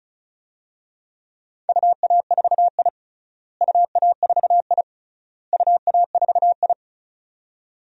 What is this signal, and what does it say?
Morse code, heard as a keyed tone, about 34 wpm, 700 Hz. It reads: UA4I UA4I UA4I
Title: DXCC-100 Call Signs - 4 Characters - Repeated 3x - 35wpm